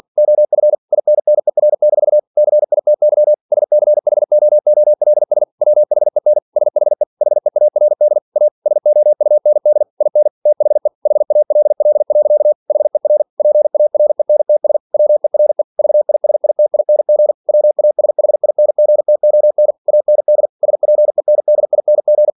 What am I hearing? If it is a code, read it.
Morse code, heard as a keyed tone, about 46 wpm, 600 Hz. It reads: OF INNER-CITY SCHOOLS WHEN SHE HEARD A SOUND IN THE HALL. HER PARENTS WERE VISITING WASHINGTON AND SPENDING
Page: Never 47wpm